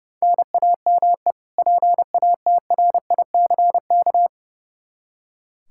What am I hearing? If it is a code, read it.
Morse code, heard as a keyed tone, about 29 wpm, 700 Hz. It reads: NAME PATRICK
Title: K1USN SST QSO Practice 30wpm